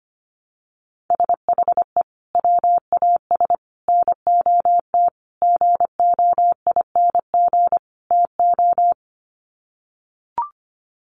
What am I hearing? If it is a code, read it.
Morse code, heard as a keyed tone, about 25 wpm, 700 Hz. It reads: SHE WAS NOT GOING TO
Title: Sets of 5 Words Encore2 25wpm